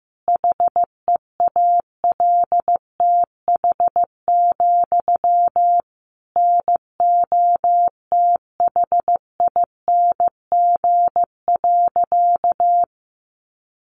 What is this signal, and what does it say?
Morse code, heard as a keyed tone, about 15 wpm, 700 Hz. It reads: HEALTH, NOTHING.
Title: Sentences from Top 900 Words 15wpm